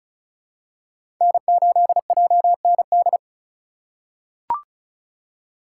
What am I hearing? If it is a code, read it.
Morse code, heard as a keyed tone, about 34 wpm, 700 Hz. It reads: N8JND
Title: USA Call Signs - 5 Characters - 3x Character Spacing - 35wpm